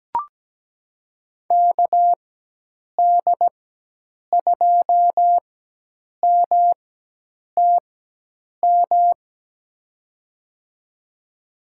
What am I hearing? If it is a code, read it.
Morse code, heard as a keyed tone, about 17 wpm, spaced as for 8 wpm, 700 Hz. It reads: KD2MTM
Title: USA Call Signs - 6 Characters - 4x Character Spacing - 17wpm